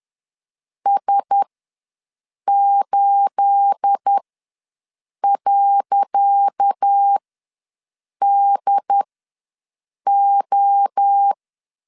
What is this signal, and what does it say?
Morse code, heard as a keyed tone, about 10 wpm, spaced as for 6 wpm, 800 Hz. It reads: S8.DO
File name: LEZIONI / 10 SIGNS / 10C DOT ws 10wpm 4213